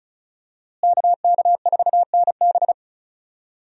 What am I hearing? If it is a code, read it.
Morse code, heard as a keyed tone, about 36 wpm, 700 Hz. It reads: KK4NB
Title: USA Call Signs - 5 Characters - Repeated 3x - 35wpm